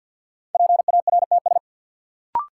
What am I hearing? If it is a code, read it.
Morse code, heard as a keyed tone, about 47 wpm, 700 Hz. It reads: PARTS
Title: Sentences from Top 1000 Words - Review Part 6 - New Word Drill - 50wpm